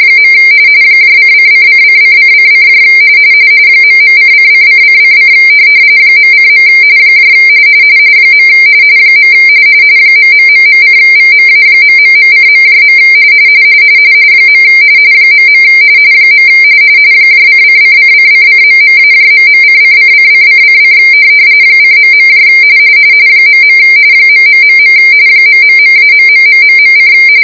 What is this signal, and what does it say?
RTTY, about 45.45 baud. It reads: . ::::::::: ..::::::: .:. . ....:VMA...:.. .. :::::::..-..:::::::: :.. .. .... ...-.....: